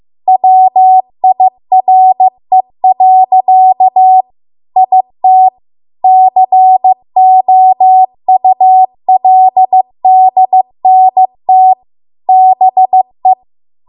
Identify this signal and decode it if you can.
Morse code, heard as a keyed tone, about 15 wpm, 750 Hz. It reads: WIRE. IT COULDNT BE